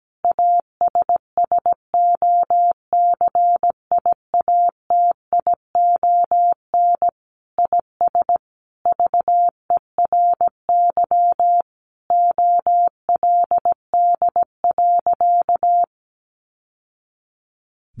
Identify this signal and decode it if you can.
Morse code, heard as a keyed tone, about 17 wpm, 700 Hz. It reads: ASSOCIATION IS VERY OLD.
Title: Sentences from Top 1100 Words 17wpm